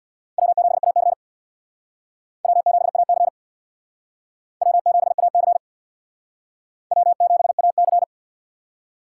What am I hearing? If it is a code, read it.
Morse code, heard as a keyed tone, about 62 wpm, 700 Hz. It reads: W7AC W7AC W7AC W7AC
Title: Speed Racing - USA 1x2 Call Signs 50wpm